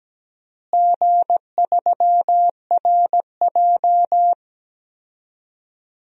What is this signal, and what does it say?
Morse code, heard as a keyed tone, about 17 wpm, 700 Hz. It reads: G3RJ
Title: DXCC-100 Call Signs - 4 Characters - Repeated 3x - 17wpm